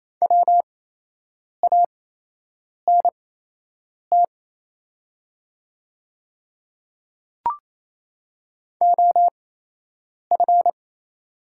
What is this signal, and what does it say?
Morse code, heard as a keyed tone, about 28 wpm, spaced as for 8 wpm, 700 Hz. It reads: WANT OF
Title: 8x Character Spacing - Top 500 Words - 28wpm